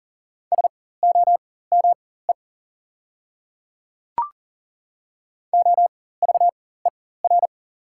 Morse code, heard as a keyed tone, about 40 wpm, spaced as for 19 wpm, 690 Hz. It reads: SOME OVER
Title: 4x Character Spacing - Top 100 Words - 40wpm